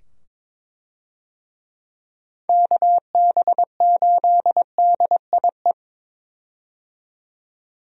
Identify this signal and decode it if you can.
Morse code, heard as a keyed tone, about 21 wpm, 700 Hz. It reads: KB8DIE
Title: USA Call Signs - 6 Characters - 22wpm